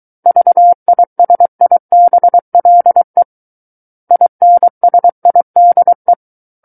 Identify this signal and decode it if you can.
Morse code, heard as a keyed tone, about 23 wpm, 700 Hz. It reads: VISIBLE INSIDE
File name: Book One / 14-Chapter 14 - 23 WPM